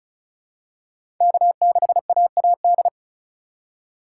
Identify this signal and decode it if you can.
Morse code, heard as a keyed tone, about 36 wpm, 700 Hz. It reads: K6AAD
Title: USA Call Signs - 5 Characters - 2x Character Spacing - 35wpm